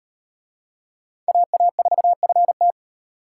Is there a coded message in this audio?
Yes: Morse code from 1.3 to 2.7 s, heard as a keyed tone, about 33 wpm, 700 Hz. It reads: AA4FT